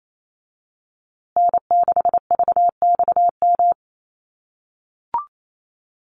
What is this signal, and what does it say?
Morse code, heard as a keyed tone, about 28 wpm, 700 Hz. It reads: N6VXM
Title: USA Call Signs Repeated 3x 28wpm